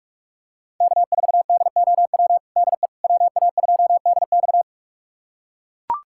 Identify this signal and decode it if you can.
Morse code, heard as a keyed tone, about 42 wpm, 700 Hz. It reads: K4DOW DE WA2DX
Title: USA Callsign DE Callsign - 5 Characters - 45wpm